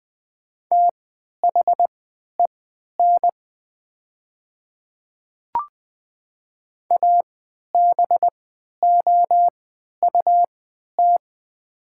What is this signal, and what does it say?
Morse code, heard as a keyed tone, about 20 wpm, spaced as for 11 wpm, 700 Hz. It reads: THEN ABOUT
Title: 3x Character Spacing - Top 200 Words - 20wpm